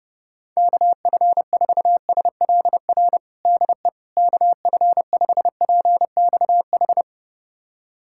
Morse code, heard as a keyed tone, about 30 wpm, 700 Hz. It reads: KF4SLR DE KF5PXH
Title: USA Callsign DE Callsign - 6 Characters - 30wpm